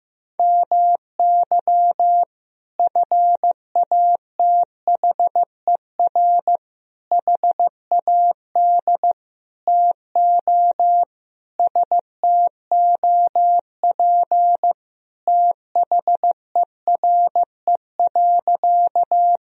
Morse code, heard as a keyed tone, about 15 wpm, 700 Hz. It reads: MY FATHER HAD TO STOP THERE.
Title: Sentences from Top 1000 Words - Review Part 1 - 15wpm